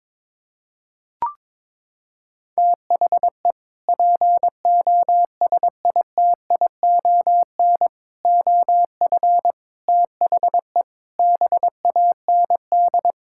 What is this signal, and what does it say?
Morse code, heard as a keyed tone, about 22 wpm, 700 Hz. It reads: THE POSITION OF THE BAND
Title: Sentences from Top 1600 Words 22wpm